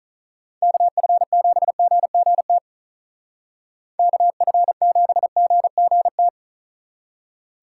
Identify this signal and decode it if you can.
Morse code, heard as a keyed tone, about 43 wpm, 700 Hz. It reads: KF7GGT KF7GGT
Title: Speed Racing - USA 2x3 Call Signs 35wpm